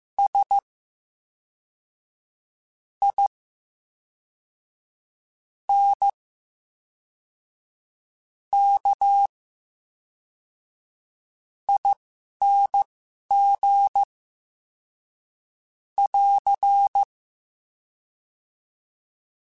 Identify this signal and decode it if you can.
Morse code, heard as a keyed tone, about 15 wpm, spaced as for 4 wpm, 800 Hz. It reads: SINKING <AR>
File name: Special INGs / Special INGs-mp3s / VerySlowToFaster Ing @10